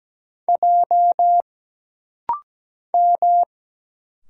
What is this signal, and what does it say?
Morse code, heard as a keyed tone, about 17 wpm, 700 Hz. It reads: J M
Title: Single Letter-Number Alt - Rapid-Fire 17wpm